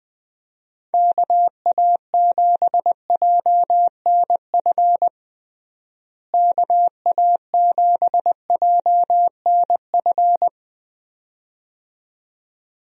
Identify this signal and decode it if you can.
Morse code, heard as a keyed tone, about 20 wpm, 700 Hz. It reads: KA7JNF KA7JNF
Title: USA Call Signs - 6 Characters - Repeated 2x - 20wpm